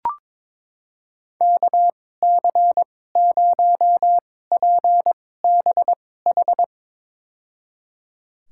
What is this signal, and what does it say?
Morse code, heard as a keyed tone, about 22 wpm, spaced as for 16 wpm, 700 Hz. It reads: KC0PBH
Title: USA Call Signs - 6 Characters - 2x Character Spacing - 22wpm